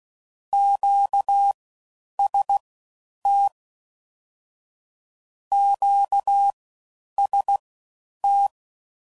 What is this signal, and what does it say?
Morse code, heard as a keyed tone, about 16 wpm, spaced as for 9 wpm, 800 Hz. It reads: QST QST
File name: Basic Sets / QST de WA7RSO / All Dits